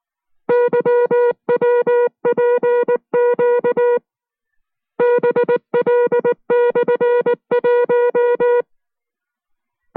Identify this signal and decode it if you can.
Morse code, heard as a keyed tone, about 19 wpm, spaced as for 15 wpm, 480 Hz. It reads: YWPQB L/1
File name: FreeFiles / CW-practice / 15WPM-side1